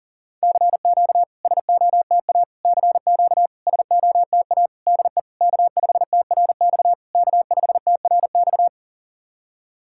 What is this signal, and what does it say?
Morse code, heard as a keyed tone, about 40 wpm, 700 Hz. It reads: CQ SOTA CQ SOTA DE K5TRX K5TRX
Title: SOTA QSO Practice 40wpm